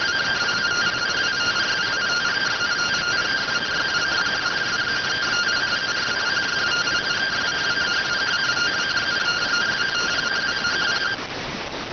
Two-tone teleprinter signal, about 170 Hz shift.